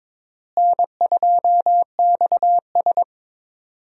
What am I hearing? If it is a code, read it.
Morse code, heard as a keyed tone, about 22 wpm, 700 Hz. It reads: N2XS
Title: USA Call Signs - 4 Characters - Repeated 2x - 22wpm